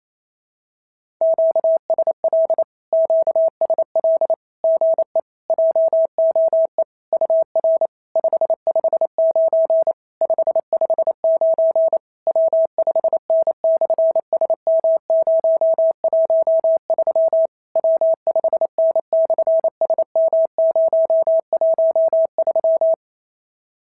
Morse code, heard as a keyed tone, about 28 wpm, 690 Hz. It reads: QSL QSL GE JOE UR 559 559 W5N/SM013 W5N/SM013
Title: SOTA QSO Practice 28wpm